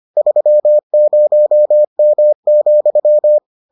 Morse code, heard as a keyed tone, about 25 wpm, 600 Hz. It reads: 30M,